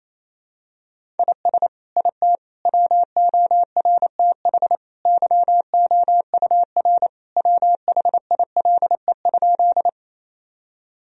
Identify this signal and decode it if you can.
Morse code, heard as a keyed tone, about 28 wpm, 700 Hz. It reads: IS IT WORTH YOUR WHILE?
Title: Sentences from Top 1000 Words - Review Part 1 - 28wpm